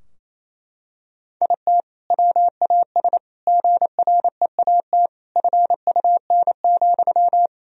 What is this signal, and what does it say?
Morse code, heard as a keyed tone, about 28 wpm, 700 Hz. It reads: IT WAS GREAT FUN,
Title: Sentences from Top 1000 Words - Review Part 4 - 28wpm